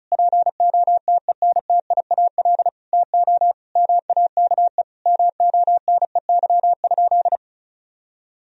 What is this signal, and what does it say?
Morse code, heard as a keyed tone, about 36 wpm, 700 Hz. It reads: POTENTIAL TO MAKE MONEY?
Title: Sentences from Top 1000 Words - Review Part 4 - 35wpm